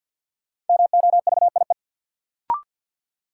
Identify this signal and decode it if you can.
Morse code, heard as a keyed tone, about 47 wpm, 700 Hz. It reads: MOVIE
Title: Sentences from Top 1000 Words - Review Part 5 - New Word Drill - 50wpm